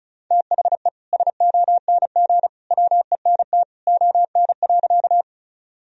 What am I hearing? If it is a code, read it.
Morse code, heard as a keyed tone, about 36 wpm, 700 Hz. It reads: THE SONG WENT ON.
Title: Sentences from Top 1000 Words - Review Part 2 - 35wpm